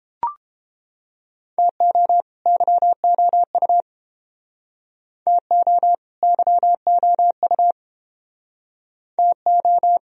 Morse code, heard as a keyed tone, about 30 wpm, 700 Hz. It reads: TO YOU TO YOU TO